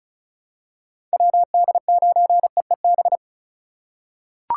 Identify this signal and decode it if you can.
Morse code, heard as a keyed tone, about 36 wpm, 700 Hz. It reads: WD9EEB
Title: USA Call Signs - 6 Characters - 2x Character Spacing - 35wpm